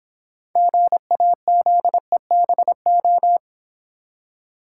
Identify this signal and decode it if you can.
Morse code, heard as a keyed tone, about 26 wpm, 690 Hz. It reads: GAZEBO